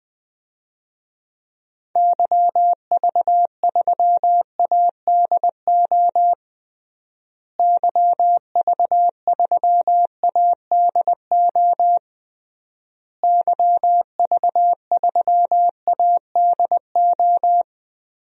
Morse code, heard as a keyed tone, about 19 wpm, 700 Hz. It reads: YV3ADO YV3ADO YV3ADO
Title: DXCC-100 Call Signs - 6 Characters - Repeated 3x - 20wpm